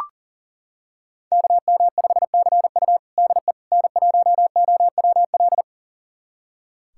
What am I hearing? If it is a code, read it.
Morse code, heard as a keyed tone, about 40 wpm, 700 Hz. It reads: KM5CU DE N1OWL
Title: USA Callsign DE Callsign - 5 Characters - 40wpm